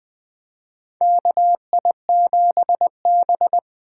Morse code, heard as a keyed tone, about 21 wpm, 700 Hz. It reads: KI7B
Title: USA Call Signs - 4 Characters - 2x Character Spacing - 20wpm